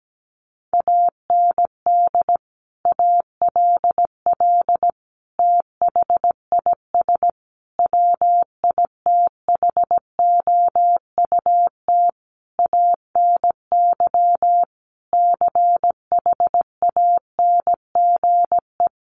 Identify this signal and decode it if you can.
Morse code, heard as a keyed tone, about 17 wpm, 700 Hz. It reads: AND ALL THIS WITHOUT ANY CHANGE
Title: Sentences from Top 1200 Words 17wpm